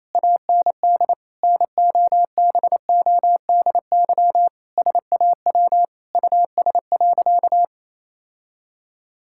Morse code, heard as a keyed tone, about 28 wpm, 700 Hz. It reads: AND NOBODY SAW US.
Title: Sentences from Top 1600 Words 28wpm